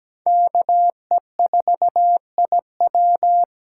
Morse code, heard as a keyed tone, about 17 wpm, 700 Hz. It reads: K E 4 I W
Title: USA Call Signs - 6 Characters - 8x Character Spacing - 17wpm